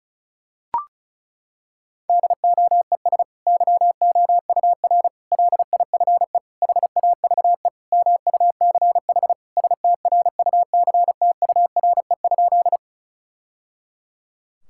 Morse code, heard as a keyed tone, about 36 wpm, 700 Hz. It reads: DOES YOUR LIFE HAVE MUCH STRUCTURE?
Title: Sentences from Top 1000 Words - Review Part 6 - 35wpm